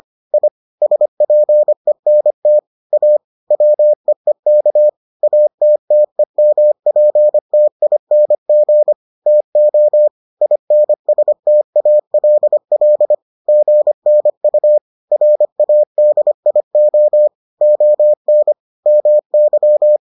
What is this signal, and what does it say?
Morse code, heard as a keyed tone, about 25 wpm, 600 Hz. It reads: I SPENT A WEEK ATTEMPTING TO INSTALL GNU RADIO ON MY